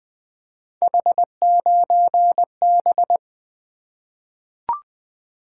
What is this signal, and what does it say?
Morse code, heard as a keyed tone, about 20 wpm, 700 Hz. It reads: H9B
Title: Three Letter-Number Combinations - 20wpm